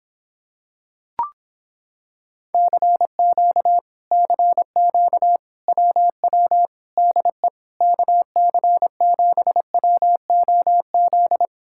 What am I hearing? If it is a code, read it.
Morse code, heard as a keyed tone, about 26 wpm, 700 Hz. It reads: CQ CQ WW DE KC7WOZ